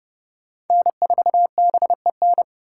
Morse code, heard as a keyed tone, about 30 wpm, 700 Hz. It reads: N4BEN
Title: USA Call Signs - 5 Characters - 4x Character Spacing - 30wpm